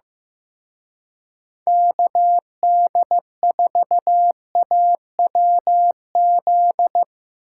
Morse code, heard as a keyed tone, about 17 wpm, 700 Hz. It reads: KD4AWZ
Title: Speed Racing - USA 2x3 Call Signs 15wpm